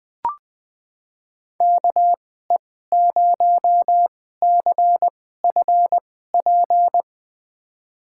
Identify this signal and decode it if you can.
Morse code, heard as a keyed tone, about 20 wpm, spaced as for 14 wpm, 700 Hz. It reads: KE0CFP
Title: USA Call Signs - 2x Character Spacing - 20wpm